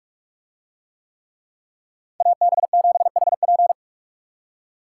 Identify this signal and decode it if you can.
Morse code, heard as a keyed tone, about 48 wpm, 700 Hz. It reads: AB7HP